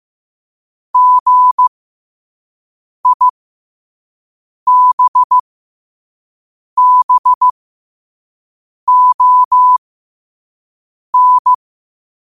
Morse code, heard as a keyed tone, about 15 wpm, spaced as for 5 wpm, 1000 Hz. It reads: GIBBON
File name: qotdfiles / 5wpm